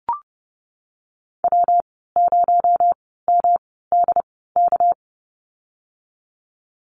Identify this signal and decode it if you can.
Morse code, heard as a keyed tone, about 30 wpm, spaced as for 17 wpm, 700 Hz. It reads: W0MDK
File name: Lesson 37g - Letter x - Call Signs 30wpm